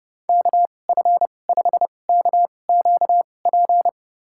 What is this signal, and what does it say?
Morse code, heard as a keyed tone, about 30 wpm, spaced as for 22 wpm, 700 Hz. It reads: KF5KQP
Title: USA Call Signs - 2x Character Spacing - 30wpm